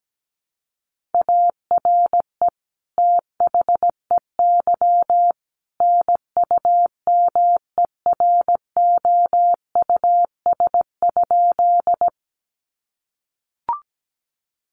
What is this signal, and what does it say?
Morse code, heard as a keyed tone, about 17 wpm, 700 Hz. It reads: ARE THEY NUMEROUS?
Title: Sentences from Top 1400 Words 17wpm